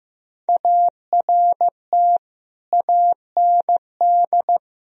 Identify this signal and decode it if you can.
Morse code, heard as a keyed tone, about 15 wpm, 700 Hz. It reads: ART AND
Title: Sentences from Top 1000 Words - Review Part 3 - 15wpm